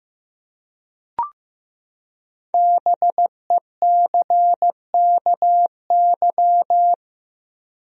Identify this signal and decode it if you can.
Morse code, heard as a keyed tone, about 16 wpm, 700 Hz. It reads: BECKY
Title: USA Common Names - 15wpm